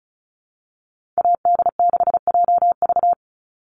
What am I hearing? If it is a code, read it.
Morse code, heard as a keyed tone, about 34 wpm, 700 Hz. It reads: AD6JV
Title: Call Signs Repeated 3x - 35wpm